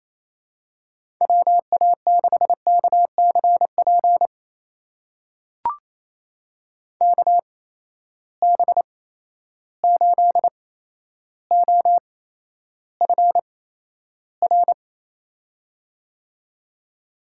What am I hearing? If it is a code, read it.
Morse code, heard as a keyed tone, about 28 wpm, spaced as for 12 wpm, 700 Hz. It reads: WA6KCP KB8OFR
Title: USA Call Signs - 6 Characters - 8x Character Spacing - 28wpm